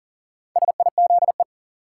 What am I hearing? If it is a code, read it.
Morse code, heard as a keyed tone, about 40 wpm, 700 Hz. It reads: SIZE
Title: Sentences from Top 400 Words - New Word Drill - 40wpm